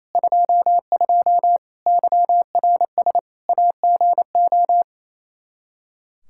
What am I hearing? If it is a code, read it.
Morse code, heard as a keyed tone, about 28 wpm, 700 Hz. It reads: 22 YRS AGO
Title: Practice QSO - Exchange 3 - More Info - 28wpm